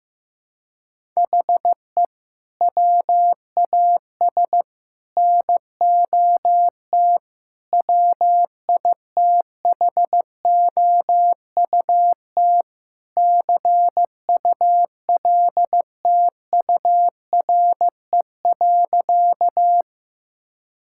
Morse code, heard as a keyed tone, about 15 wpm, 700 Hz. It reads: HE WAS NOT WITHOUT CULTURE.
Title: Sentences from Top 1000 Words - Review Part 6 - 15wpm